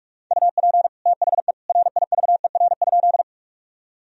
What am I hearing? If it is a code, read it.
Morse code, heard as a keyed tone, about 46 wpm, 700 Hz. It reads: UP THE RIVER?
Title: Sentences from Top 1000 Words - Review Part 1 - 45wpm